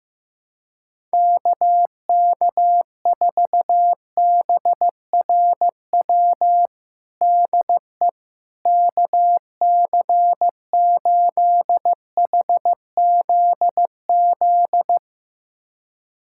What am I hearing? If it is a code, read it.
Morse code, heard as a keyed tone, about 15 wpm, 700 Hz. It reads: KK4BRW DE KC8HZZ
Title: USA Callsign DE Callsign - 6 Characters - 15wpm